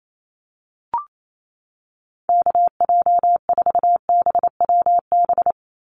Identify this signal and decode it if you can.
Morse code, heard as a keyed tone, about 28 wpm, 700 Hz. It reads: KJ4BWB
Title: USA Call Signs 28wpm